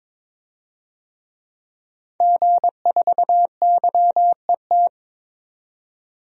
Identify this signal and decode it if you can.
Morse code, heard as a keyed tone, about 22 wpm, 700 Hz. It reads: G4YET
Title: DXCC-100 Call Signs - 5 Characters - 22wpm